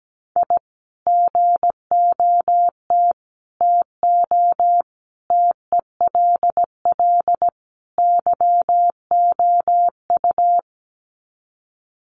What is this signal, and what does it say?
Morse code, heard as a keyed tone, about 17 wpm, 700 Hz. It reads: I GOT TO TELL YOU
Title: Sets of 5 Words Encore2 17wpm